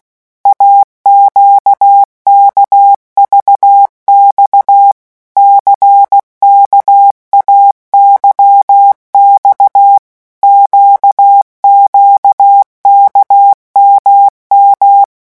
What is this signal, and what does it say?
Morse code, heard as a keyed tone, about 16 wpm, 800 Hz. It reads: AQKVX CKAYX QQKMM
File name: morse mp3 / revision2-vitesse 15 mots